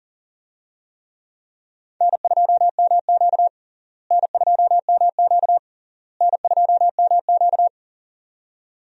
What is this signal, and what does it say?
Morse code, heard as a keyed tone, about 37 wpm, 700 Hz. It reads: N2MQ N2MQ N2MQ
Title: USA Call Signs - 4 Characters - Repeated 3x - 40wpm